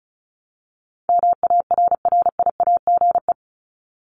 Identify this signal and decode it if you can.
Morse code, heard as a keyed tone, about 34 wpm, 700 Hz. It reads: MARRIAGE
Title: Top 751-1000 Words 35wpm